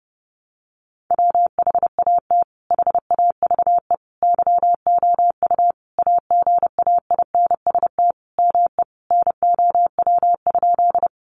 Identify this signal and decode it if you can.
Morse code, heard as a keyed tone, about 30 wpm, 700 Hz. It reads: WHAT HAVE YOU AGAINST ME NOW?
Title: Sentences from Top 400 Words 30wpm